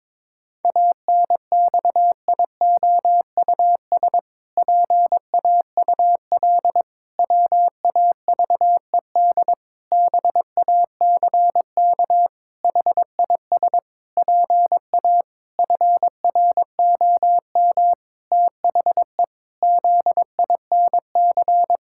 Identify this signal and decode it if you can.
Morse code, heard as a keyed tone, about 22 wpm, 700 Hz. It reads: ANXIOUS PAUL WAVED BACK HIS PA FROM THE ZINC
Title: Pangram Challenge v2 - 22wpm